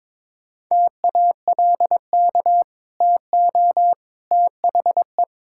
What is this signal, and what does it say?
Morse code, heard as a keyed tone, about 22 wpm, 700 Hz. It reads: TALK TO THE
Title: Sentences from Top 1600 Words 22wpm